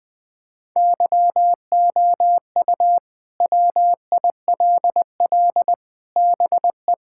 Morse code, heard as a keyed tone, about 20 wpm, 700 Hz. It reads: YOU WILL BE
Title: Sentences with 4-Word Phrases - Spoken Beginning - 20wpm